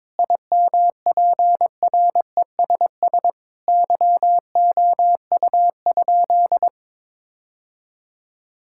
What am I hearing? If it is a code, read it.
Morse code, heard as a keyed tone, about 22 wpm, 700 Hz. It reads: IMPRESS YOU?
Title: Sentences from Top 1600 Words 22wpm